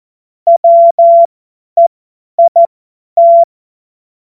Morse code, heard as a keyed tone, about 14 wpm, spaced as for 10 wpm, 680 Hz. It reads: WEIT
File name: Morsekurs / Witze / Witz 06. Januar 2026 50 F70